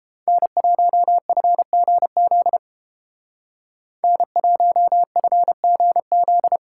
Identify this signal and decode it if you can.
Morse code, heard as a keyed tone, about 32 wpm, 700 Hz. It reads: N1FGZ N1FGZ